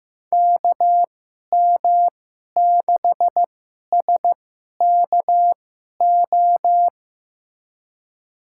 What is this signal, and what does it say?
Morse code, heard as a keyed tone, about 15 wpm, spaced as for 11 wpm, 700 Hz. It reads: KM6SKO
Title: USA Call Signs - 2x Character Spacing - 15wpm